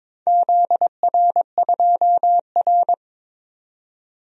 Morse code, heard as a keyed tone, about 22 wpm, 700 Hz. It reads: ZR2R
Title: DXCC-100 Call Signs - 4 Characters - 22wpm